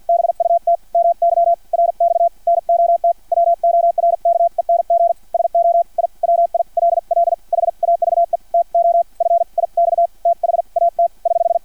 Morse code, heard as a keyed tone, about 50 wpm, spaced as for 43 wpm, 680 Hz. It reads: BUT MY RX NOT WORKENG SO I WILL HAVE TO FIX THAT <HH>
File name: archived / CRSnet